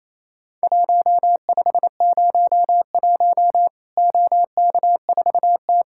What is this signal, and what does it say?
Morse code, heard as a keyed tone, about 28 wpm, 700 Hz. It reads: 1501 OK4T
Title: Worked All Europe DX Contest - QTC Exchange - 28wpm